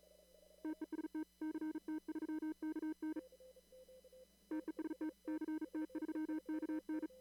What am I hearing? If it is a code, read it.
Morse code, heard as a keyed tone, about 39 wpm, 310 Hz. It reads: TEST CT3KN TEST CT3KN
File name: CW / kt5j wpx cw-op-k5pi